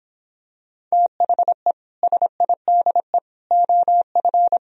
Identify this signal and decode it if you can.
Morse code, heard as a keyed tone, about 26 wpm, 700 Hz. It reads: THE SIDE OF